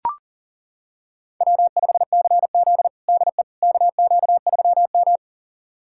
Morse code, heard as a keyed tone, about 40 wpm, 700 Hz. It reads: W5CZ DE KQ3M
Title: USA Callsign DE Callsign - 4 Characters - 40wpm